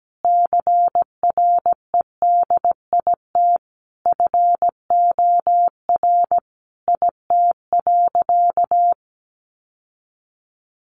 Morse code, heard as a keyed tone, about 17 wpm, 700 Hz. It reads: CREDIT FOR IT.
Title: Sentences from Top 1000 Words 17wpm